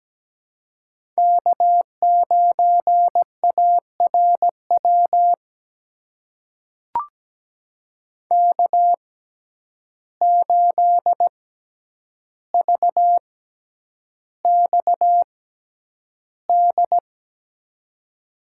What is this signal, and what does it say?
Morse code, heard as a keyed tone, about 17 wpm, spaced as for 9 wpm, 700 Hz. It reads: K9ARW K8VXD
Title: USA Call Signs - 5 Characters - 6x Character Spacing - 17wpm